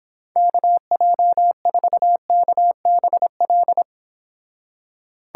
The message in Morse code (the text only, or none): KJ4KBL